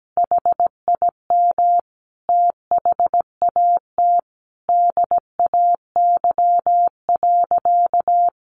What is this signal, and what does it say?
Morse code, heard as a keyed tone, about 17 wpm, 700 Hz. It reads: HIM THAT DAY.
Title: Sentences from Top 1300 Words 17wpm